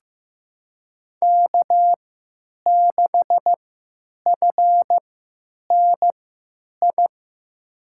Morse code, heard as a keyed tone, about 16 wpm, spaced as for 9 wpm, 700 Hz. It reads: K6FNI
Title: USA Call Signs - 5 Characters - 3x Character Spacing - 15wpm